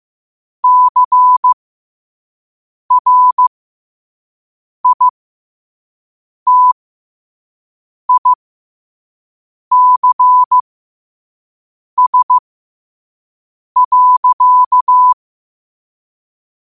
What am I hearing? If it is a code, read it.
Morse code, heard as a keyed tone, about 15 wpm, spaced as for 5 wpm, 1000 Hz. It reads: CRITICS.
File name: qotdfiles / 5wpm